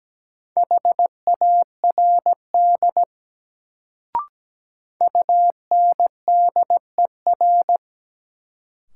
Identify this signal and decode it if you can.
Morse code, heard as a keyed tone, about 17 wpm, 700 Hz. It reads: HARD UNDER
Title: Sentences from Top 300 Words - New Word Drill - 17wpm